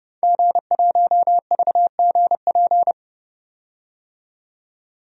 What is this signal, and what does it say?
Morse code, heard as a keyed tone, about 30 wpm, 700 Hz. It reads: G1VGP
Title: DXCC-100 Call Signs - 5 Characters - 30wpm